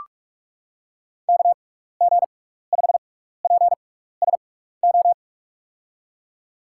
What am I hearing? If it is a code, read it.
Morse code, heard as a keyed tone, about 44 wpm, spaced as for 15 wpm, 700 Hz. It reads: KG5PSO
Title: USA Call Signs - 6 Characters - 6x Character Spacing - 45wpm